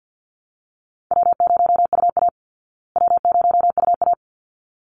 Morse code, heard as a keyed tone, about 48 wpm, 700 Hz. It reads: W0UA W0UA
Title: Easy Call Signs Repeated 2x - 50wpm